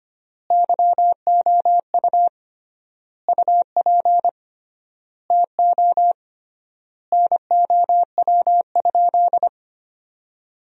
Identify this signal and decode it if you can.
Morse code, heard as a keyed tone, about 25 wpm, 700 Hz. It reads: YOU UP TO NOW?
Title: 3x Word-Spacing - Sentences from Top 100 Words - 25wpm